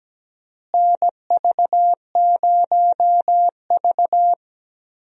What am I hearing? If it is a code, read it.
Morse code, heard as a keyed tone, about 17 wpm, 700 Hz. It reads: NV0V
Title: USA Call Signs - 4 Characters - Repeated 2x - 17wpm